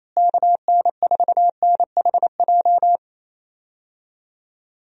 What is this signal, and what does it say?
Morse code, heard as a keyed tone, about 28 wpm, 700 Hz. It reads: KN4NHJ
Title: USA Call Signs - 6 Characters - Repeated 2x - 28wpm